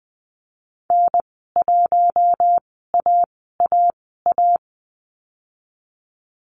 Morse code, heard as a keyed tone, about 20 wpm, spaced as for 14 wpm, 700 Hz. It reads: N1AAA
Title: 042 Lesson 8g - Number 1 - Call Signs 20wpm